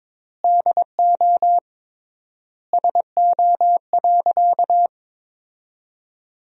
Morse code, heard as a keyed tone, about 22 wpm, 700 Hz. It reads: DO SO.
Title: 3x Word-Spacing - Sentences from Top 500 Words - 22wpm